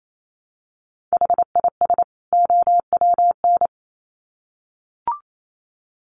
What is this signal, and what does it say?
Morse code, heard as a keyed tone, about 28 wpm, 700 Hz. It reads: HIS OWN
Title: Sets of 2 Words 28wpm